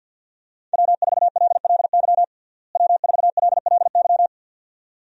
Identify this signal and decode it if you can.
Morse code, heard as a keyed tone, about 51 wpm, 700 Hz. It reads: W4LLY W4LLY
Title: USA Call Signs - 5 Characters - Repeated 2x - 50wpm